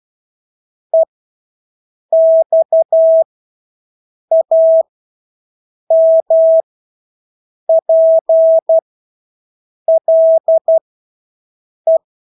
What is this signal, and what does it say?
Morse code, heard as a keyed tone, about 12 wpm, spaced as for 6 wpm, 650 Hz. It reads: EXAMPLE
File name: Engelske ord 12-6 wpm